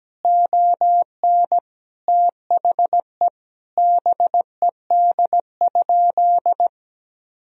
Morse code, heard as a keyed tone, about 17 wpm, 700 Hz. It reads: ON THE BED?
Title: Sentences from Top 1000 Words - Review Part 4 - 17wpm